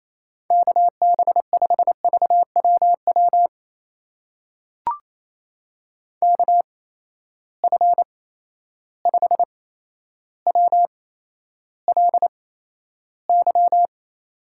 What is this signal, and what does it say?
Morse code, heard as a keyed tone, about 28 wpm, spaced as for 12 wpm, 700 Hz. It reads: KB5VWW KF5WLY
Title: USA Call Signs - 6 Characters - 8x Character Spacing - 28wpm